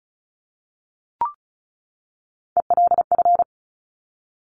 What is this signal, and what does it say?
Morse code, heard as a keyed tone, about 33 wpm, 700 Hz. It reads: ELF
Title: Top Three Letter Combinations in Words 35wpm